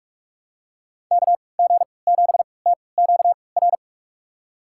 Morse code, heard as a keyed tone, about 44 wpm, spaced as for 25 wpm, 700 Hz. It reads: KG7TQR
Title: USA Call Signs - 6 Characters - 3x Character Spacing - 45wpm